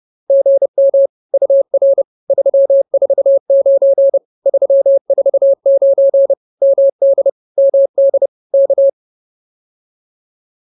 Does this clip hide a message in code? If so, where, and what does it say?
Yes: Morse code from 0.3 to 8.9 s, heard as a keyed tone, about 30 wpm, 550 Hz. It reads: GM UR 349 349 MD MD K